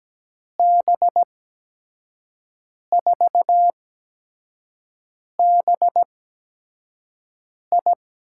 Morse code, heard as a keyed tone, about 17 wpm, spaced as for 5 wpm, 700 Hz. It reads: B4BI
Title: USA Call Signs - 8x Character Spacing - 17wpm